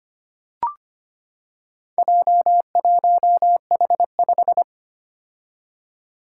Morse code, heard as a keyed tone, about 25 wpm, 700 Hz. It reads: J1H5
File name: practice / 25wpm / Four Letter-Number Combinations - 25wpm